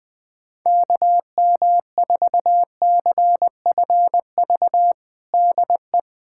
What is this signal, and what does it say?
Morse code, heard as a keyed tone, about 21 wpm, 700 Hz. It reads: KM4CFV DE
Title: USA Callsign DE Callsign - 6 Characters - 20wpm